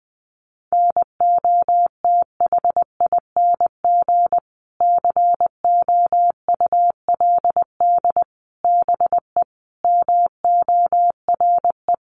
Morse code, heard as a keyed tone, about 20 wpm, 700 Hz. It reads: NOTHING COULD BE MORE
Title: Sentences from Top 500 Words 20wpm